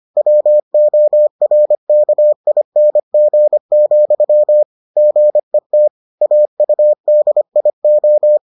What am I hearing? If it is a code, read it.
Morse code, heard as a keyed tone, about 25 wpm, 600 Hz. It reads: WORKING, GET AUDIO